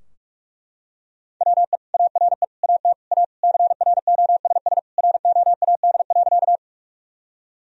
Morse code, heard as a keyed tone, about 42 wpm, 700 Hz. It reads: WE ARE AT A CROSS ROAD.
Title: Sentences from Top 1000 Words - Review Part 2 - 45wpm